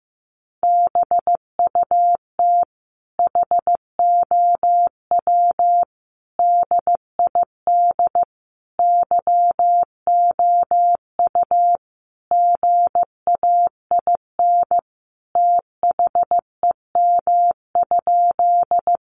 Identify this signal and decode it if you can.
Morse code, heard as a keyed tone, about 15 wpm, 700 Hz. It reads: BUT HOW DID YOU GAIN THEM?
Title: Sentences from Top 1000 Words 15wpm